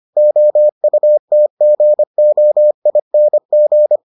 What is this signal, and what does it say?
Morse code, heard as a keyed tone, about 25 wpm, 600 Hz. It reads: OUTGOING